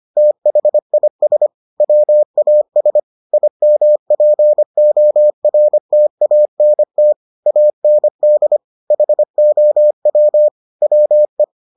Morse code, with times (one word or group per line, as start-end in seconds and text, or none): 0.2-1.5 THIS
1.8-3.0 WAS
3.3-7.1 IMPORTANT
7.5-8.6 AND
8.9-10.5 HOW
10.8-11.4 WE